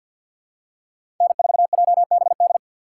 Morse code, heard as a keyed tone, about 52 wpm, 700 Hz. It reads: N4JBD
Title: USA Call Signs - 5 Characters - 2x Character Spacing - 50wpm